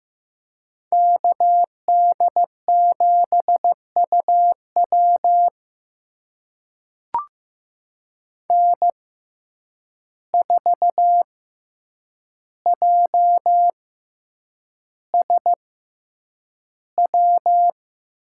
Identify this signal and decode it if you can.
Morse code, heard as a keyed tone, about 15 wpm, spaced as for 8 wpm, 700 Hz. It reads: KD7UW N4JSW
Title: USA Call Signs - 5 Characters - 6x Character Spacing - 15wpm